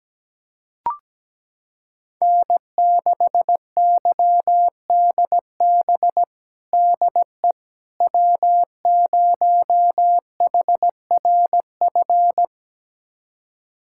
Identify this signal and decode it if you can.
Morse code, heard as a keyed tone, about 17 wpm, 700 Hz. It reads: N6YDB DE W0HRF
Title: USA Callsign DE Callsign - 5 Characters - 17wpm